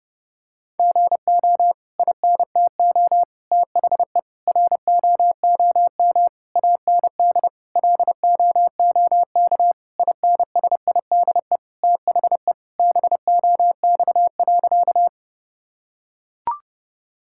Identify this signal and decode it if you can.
Morse code, heard as a keyed tone, about 30 wpm, 700 Hz. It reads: GO INTO THE ROOM AND LOOK INSIDE THE BOX.
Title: Sentences from Top 1000 Words - Review Part 6 - 30wpm